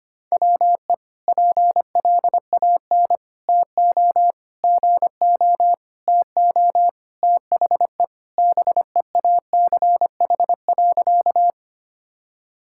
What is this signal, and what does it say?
Morse code, heard as a keyed tone, about 25 wpm, 700 Hz. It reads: WE PLAN TO GO TO THE BEACH.
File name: practice / 25wpm / Sentences with 4-Word Phrases - 25wpm